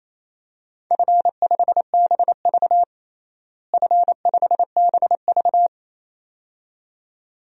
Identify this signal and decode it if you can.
Morse code, heard as a keyed tone, about 28 wpm, 700 Hz. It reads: F5BV F5BV
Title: DXCC-100 Call Signs - 4 Characters - Repeated 2x - 28wpm